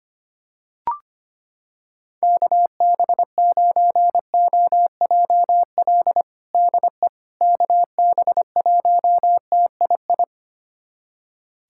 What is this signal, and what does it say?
Morse code, heard as a keyed tone, about 25 wpm, 700 Hz. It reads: KB9OJL DE KB1TII
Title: USA Callsign DE Callsign 25wpm